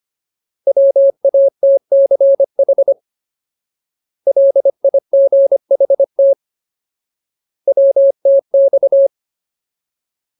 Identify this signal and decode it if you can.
Morse code, heard as a keyed tone, about 25 wpm, 550 Hz. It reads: WATCH LIGHT WTX
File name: Ultimate Word List / 25WPM Ultimate